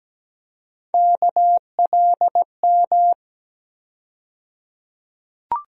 Morse code, heard as a keyed tone, about 17 wpm, 700 Hz. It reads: KLM
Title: USA Call Sign Postfixes - 3 Letters 17wpm